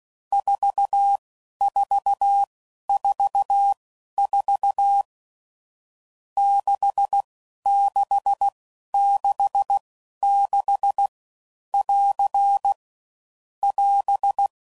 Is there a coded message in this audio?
Yes: Morse code from 0.3 to 14.5 s, heard as a keyed tone, about 16 wpm, spaced as for 11 wpm, 800 Hz. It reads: 4444 6666 <AR> <AS>